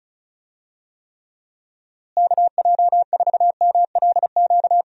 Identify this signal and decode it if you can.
Morse code, heard as a keyed tone, about 36 wpm, 700 Hz. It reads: KJ4MLQ
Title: USA Call Signs - 6 Characters - Repeated 3x - 35wpm